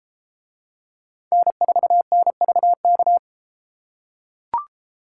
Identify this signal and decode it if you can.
Morse code, heard as a keyed tone, about 31 wpm, 700 Hz. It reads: N4NVK